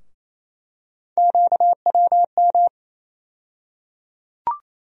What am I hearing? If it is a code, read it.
Morse code, heard as a keyed tone, about 28 wpm, 700 Hz. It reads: QWM
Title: USA Call Sign Postfixes - 3 Letters 28wpm